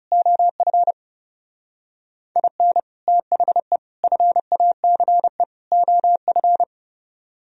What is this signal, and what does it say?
Morse code, heard as a keyed tone, about 35 wpm, 700 Hz. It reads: OF IN THE FACE OF
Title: Speed-Racing - Sets of 4 Words 30wpm